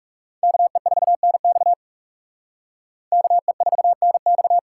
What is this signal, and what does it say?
Morse code, heard as a keyed tone, about 37 wpm, 700 Hz. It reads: KE4NX KE4NX
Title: Speed Racing - USA Call Signs 30wpm